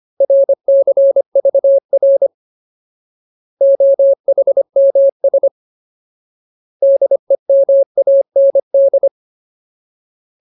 Morse code, heard as a keyed tone, about 25 wpm, 550 Hz. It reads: RCVR OHMS DEMAND
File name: Ultimate Word List / 25WPM Ultimate